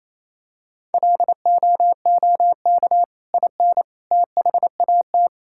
Morse code, heard as a keyed tone, about 28 wpm, 700 Hz. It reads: LOOK IN THAT
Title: Sentences from Top 1000 Words - Review Part 5 - 28wpm